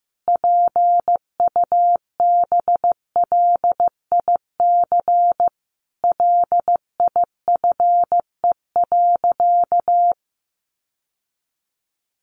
Morse code, heard as a keyed tone, about 15 wpm, 700 Hz. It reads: PUBLIC LIFE.
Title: Sentences from Top 600 Words 15wpm